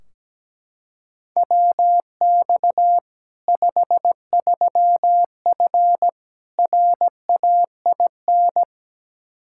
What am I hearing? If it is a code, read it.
Morse code, heard as a keyed tone, about 17 wpm, 700 Hz. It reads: WX 53F RAIN
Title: Practice QSO - Exchange 2 - Info - 17wpm